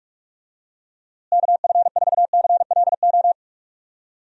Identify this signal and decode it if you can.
Morse code, heard as a keyed tone, about 41 wpm, 700 Hz. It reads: KF4CLO
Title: USA Call Signs - 6 Characters - Repeated 3x - 45wpm